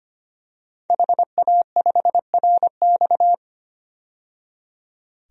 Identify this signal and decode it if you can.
Morse code, heard as a keyed tone, about 25 wpm, 700 Hz. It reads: HA5RX
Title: DXCC-100 Call Signs - 5 Characters - 25wpm